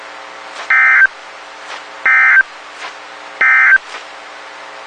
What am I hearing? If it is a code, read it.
Emergency Alert System (SAME) tones, heard as harsh buzzing bursts. It is heard from 0.6 to 3.9 s.
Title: WXK91 NOAA Weather Radio